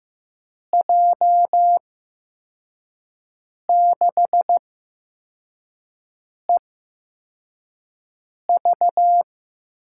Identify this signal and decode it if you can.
Morse code, heard as a keyed tone, about 15 wpm, spaced as for 4 wpm, 700 Hz. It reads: J6EV
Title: USA Call Signs - 6 Characters - 8x Character Spacing - 15wpm